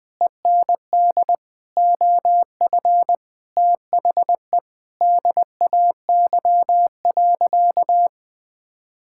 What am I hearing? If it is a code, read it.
Morse code, heard as a keyed tone, about 20 wpm, 700 Hz. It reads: END OF THE DAY.
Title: Sentences from Top 1000 Words - Review Part 6 - 20wpm